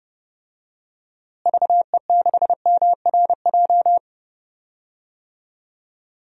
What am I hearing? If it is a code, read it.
Morse code, heard as a keyed tone, about 32 wpm, 700 Hz. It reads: VE6MRJ
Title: DXCC-100 Call Signs - 6 Characters - Repeated 2x - 30wpm